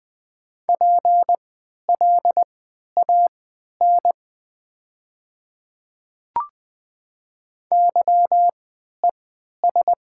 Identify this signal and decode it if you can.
Morse code, heard as a keyed tone, about 20 wpm, spaced as for 11 wpm, 700 Hz. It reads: PLAN YES
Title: 3x Character Spacing - Top 500 Words - 20wpm